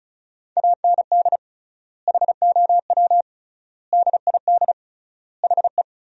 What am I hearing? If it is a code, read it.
Morse code, heard as a keyed tone, about 34 wpm, 700 Hz. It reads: AND HOW DID HE
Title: 3x Word-Spacing - Sentences from Top 400 Words - 35wpm